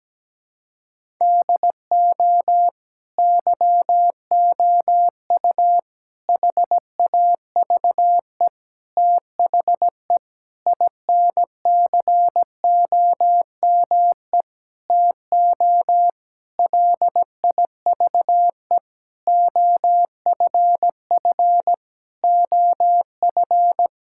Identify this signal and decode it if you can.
Morse code, heard as a keyed tone, about 17 wpm, 700 Hz. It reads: DO YOU HAVE THE INCOME TO LIVE OFF OF
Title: Sentences from Top 1000 Words - Review Part 5 - 17wpm